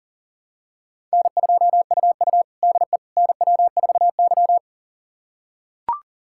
Morse code, heard as a keyed tone, about 40 wpm, 690 Hz. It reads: N2UU DE NW4Y
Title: USA Callsign DE Callsign - 4 Characters - 40wpm